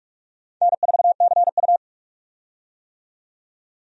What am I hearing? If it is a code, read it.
Morse code, heard as a keyed tone, about 46 wpm, 700 Hz. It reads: N4CU
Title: USA Call Signs - 4 Characters - 45wpm